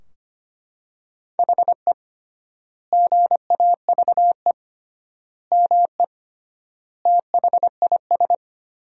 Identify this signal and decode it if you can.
Morse code, heard as a keyed tone, about 25 wpm, 700 Hz. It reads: HE GAVE ME THIS
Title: 3x Word-Spacing - Sentences from Top 500 Words - 25wpm